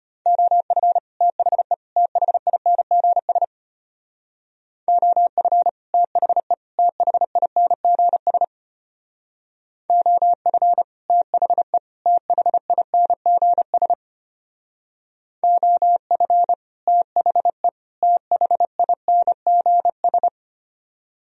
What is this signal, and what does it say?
Morse code, heard as a keyed tone, about 32 wpm, 700 Hz. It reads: OF THE THINGS OF THE THINGS OF THE THINGS OF THE THINGS